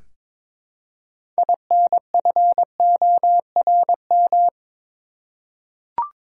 Morse code, heard as a keyed tone, about 22 wpm, 700 Hz. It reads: INFORM
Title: Six Letter Words - 22wpm